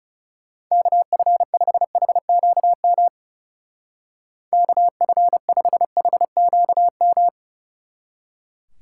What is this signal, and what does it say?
Morse code, heard as a keyed tone, about 36 wpm, 700 Hz. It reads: KF5HQM KF5HQM
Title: Speed Racing - USA 2x3 Call Signs 30wpm